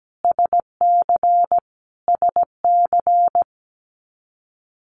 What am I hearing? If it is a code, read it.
Morse code, heard as a keyed tone, about 17 wpm, 700 Hz. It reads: SC SC
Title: US States Repeated 3x 17wpm